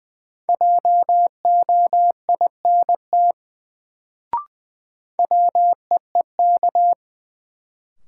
Morse code, heard as a keyed tone, about 20 wpm, 700 Hz. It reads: JOINT WEEK
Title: Sets of 5 Words Encore - New Word Drill - 20wpm